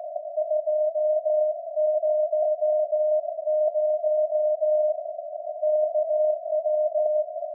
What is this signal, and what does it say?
Morse code, heard as a keyed tone, about 18 wpm, 610 Hz. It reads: 200 KW
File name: iz7slz / VLF / SAQ jun 2019